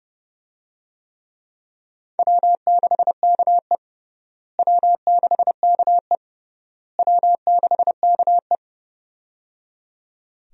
Morse code, heard as a keyed tone, about 30 wpm, 700 Hz. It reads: W6KE W6KE W6KE
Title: USA Call Signs - 4 Characters - Repeated 3x - 30wpm